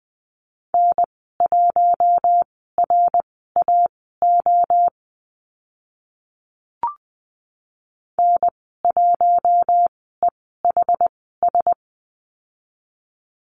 Morse code, heard as a keyed tone, about 20 wpm, spaced as for 14 wpm, 700 Hz. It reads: N1RAO N1EHS
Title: 062 Lesson 11g - Letter h - Call Signs 20wpm